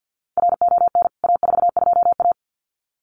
Morse code, heard as a keyed tone, about 50 wpm, 700 Hz. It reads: RON A42A
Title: CWT Exchange 50wpm